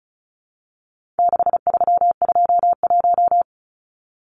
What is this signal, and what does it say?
Morse code, heard as a keyed tone, about 34 wpm, 700 Hz. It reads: 6321
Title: Numbers - 4-Digit 35wpm